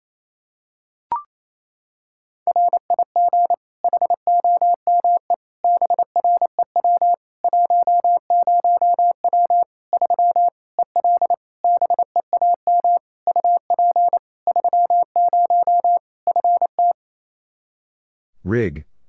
Morse code, heard as a keyed tone, about 27 wpm, 700 Hz. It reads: RIG HOME BREW 10W 3 EL BEAM UP 30 FT
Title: Practice QSO - Exchange 2 - Info - 28wpm